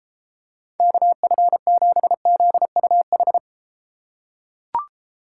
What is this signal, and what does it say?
Morse code, heard as a keyed tone, about 33 wpm, 700 Hz. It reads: KF7ZUH